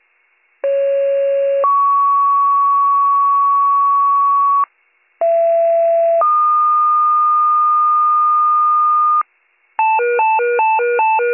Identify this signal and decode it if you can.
Two-tone page, heard as a one-second tone, then a three-second tone, about 550 then 1120 Hz, about 670 then 1220 Hz. It is heard from 0.6 to 9.2 s.